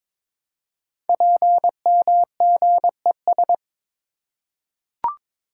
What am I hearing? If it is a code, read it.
Morse code, heard as a keyed tone, about 22 wpm, 700 Hz. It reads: PMGES
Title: Code Groups - 5 Letters - 22wpm